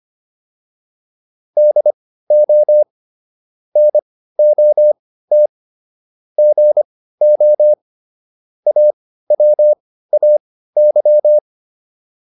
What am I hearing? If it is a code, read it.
Morse code, heard as a keyed tone, about 25 wpm, spaced as for 15 wpm, 600 Hz. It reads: DO NOT GO AWAY